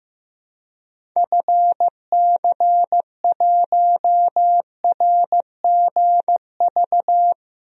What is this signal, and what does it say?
Morse code, heard as a keyed tone, about 15 wpm, 700 Hz. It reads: FC1RGV
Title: DXCC-100 Call Signs - 6 Characters - 15wpm